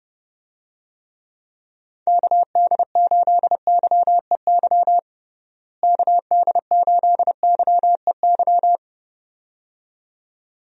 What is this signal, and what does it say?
Morse code, heard as a keyed tone, about 32 wpm, 700 Hz. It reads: KD8YEY KD8YEY
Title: USA Call Signs - 6 Characters - Repeated 2x - 30wpm